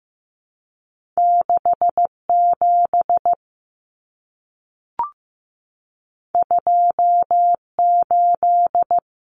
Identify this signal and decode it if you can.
Morse code, heard as a keyed tone, about 15 wpm, 700 Hz. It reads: 6 7 2 8
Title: Common Cut Numbers - 2-Digit 15wpm